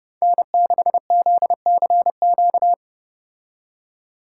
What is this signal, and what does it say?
Morse code, heard as a keyed tone, about 31 wpm, 700 Hz. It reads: N6ZCQ